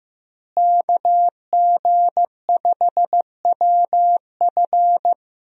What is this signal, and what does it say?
Morse code, heard as a keyed tone, about 15 wpm, 700 Hz. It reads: KG5WF
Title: USA Call Signs - 5 Characters - Repeated 3x - 15wpm